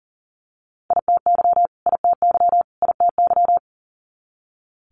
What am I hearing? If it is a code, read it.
Morse code, heard as a keyed tone, about 42 wpm, 700 Hz. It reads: ITY ITY ITY
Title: Multi-letter Phonemes Repeated 3x 40wpm